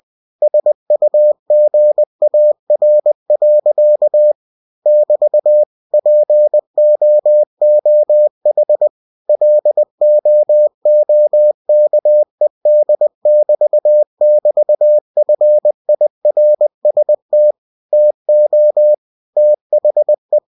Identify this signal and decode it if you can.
Morse code, heard as a keyed tone, about 20 wpm, 600 Hz. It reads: SUGAR. <BT> POOH LOOKED <BT> <BT> FIRST TO THE
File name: cw-books / winnie-the-pooh / winnie-20wpm-page-0047